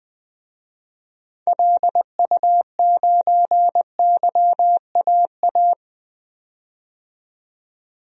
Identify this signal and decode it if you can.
Morse code, heard as a keyed tone, about 20 wpm, 700 Hz. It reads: LU9YAA
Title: DXCC-100 Call Signs - 6 Characters - 20wpm